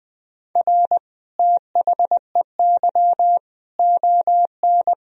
Morse code, heard as a keyed tone, about 20 wpm, 700 Hz. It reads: R THEY ON
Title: Sentences from Top 500 Words and Abbreviations 20wpm